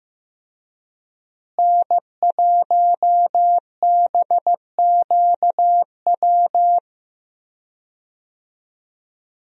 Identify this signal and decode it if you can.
Morse code, heard as a keyed tone, about 15 wpm, 700 Hz. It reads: N1BQW
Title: USA Call Signs - 5 Characters - Repeated 2x - 15wpm